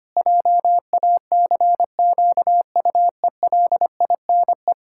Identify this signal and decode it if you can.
Morse code, heard as a keyed tone, about 25 wpm, 700 Hz. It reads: JACQUELINE
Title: Practice QSO - Exchange 1 - Beginning - 25wpm